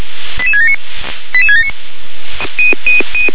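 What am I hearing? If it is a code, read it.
Five-tone selective call (ZVEI), heard as quick runs of five tones, tones of about 70 ms. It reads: 88679 88679
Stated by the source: ZVEI five-tone selective call received with an SDR.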